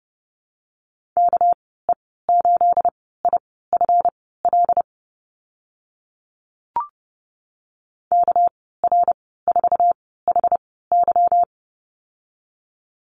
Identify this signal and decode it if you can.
Morse code, heard as a keyed tone, about 30 wpm, spaced as for 17 wpm, 700 Hz. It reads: KE8IFL KR4HY
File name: Lesson 37g - Letter x - Call Signs 30wpm